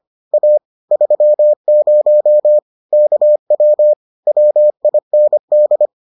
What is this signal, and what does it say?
Morse code, heard as a keyed tone, about 25 wpm, 600 Hz. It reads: A 30 KW WIND